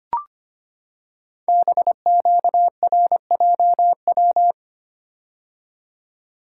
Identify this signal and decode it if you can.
Morse code, heard as a keyed tone, about 25 wpm, 700 Hz. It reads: BQRJW
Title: Code Groups - 5 Letters - 25wpm